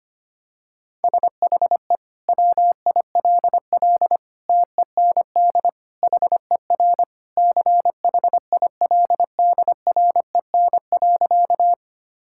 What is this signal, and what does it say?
Morse code, heard as a keyed tone, about 25 wpm, 700 Hz. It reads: SHE WILL TEND HER CHILDREN.
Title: Sentences from Top 1000 Words - Review Part 5 - 25wpm